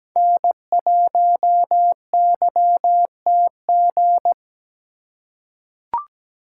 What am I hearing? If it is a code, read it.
Morse code, heard as a keyed tone, about 17 wpm, 700 Hz. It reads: N1YTG
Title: USA Call Signs - 5 Characters - 4x Character Spacing - 17wpm